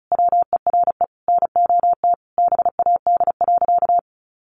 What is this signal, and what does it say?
Morse code, heard as a keyed tone, about 34 wpm, 700 Hz. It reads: WERE NOT BAD.
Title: Sentences from Top 900 Words 35wpm